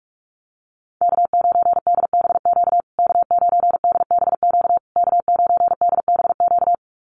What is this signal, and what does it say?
Morse code, heard as a keyed tone, about 44 wpm, 700 Hz. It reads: K9DBQ K9DBQ K9DBQ
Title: USA Call Signs Repeated 3x 45wpm